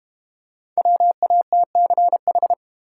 Morse code, heard as a keyed tone, about 30 wpm, 700 Hz. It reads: WATCH
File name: Easy Words - Speed-Racing with Hints - 25wpm